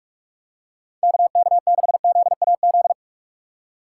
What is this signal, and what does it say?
Morse code, heard as a keyed tone, about 46 wpm, 700 Hz. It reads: KK6ZAZ
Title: USA Call Signs - 6 Characters - 4x Character Spacing - 45wpm